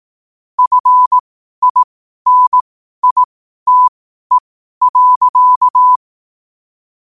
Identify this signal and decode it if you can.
Morse code, heard as a keyed tone, about 18 wpm, spaced as for 12 wpm, 1000 Hz. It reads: FINITE.
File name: qotdfiles / 12wpm